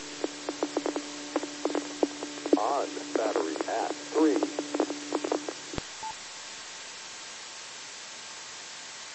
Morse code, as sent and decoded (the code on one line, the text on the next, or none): none